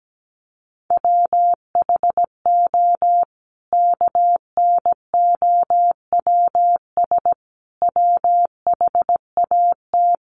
Morse code, heard as a keyed tone, about 17 wpm, 700 Hz. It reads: WHO KNOWS WHAT
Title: A Sound Of Thunder 17wpm